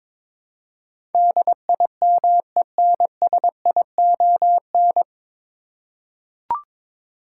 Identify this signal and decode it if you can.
Morse code, heard as a keyed tone, about 22 wpm, 700 Hz. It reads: DIMENSION
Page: Morse Code Ninja practice 22wpm